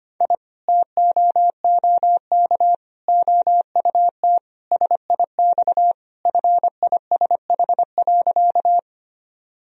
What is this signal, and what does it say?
Morse code, heard as a keyed tone, about 25 wpm, 700 Hz. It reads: I TOOK OUT SIX FISH.
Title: Sentences from Top 1000 Words - Review Part 2 - 25wpm